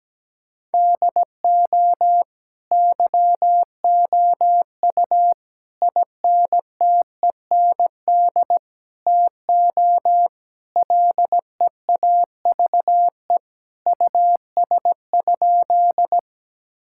Morse code, heard as a keyed tone, about 17 wpm, 700 Hz. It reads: DO YOU INTEND TO LEAVE US?
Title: Sentences from Top 1600 Words 17wpm